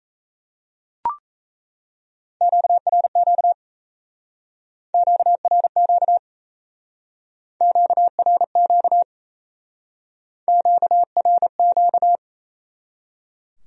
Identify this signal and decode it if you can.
Morse code, heard as a keyed tone, about 36 wpm, 700 Hz. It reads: QRQ QRQ QRQ QRQ